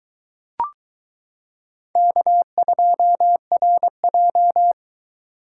Morse code, heard as a keyed tone, about 22 wpm, 700 Hz. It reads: K2RJ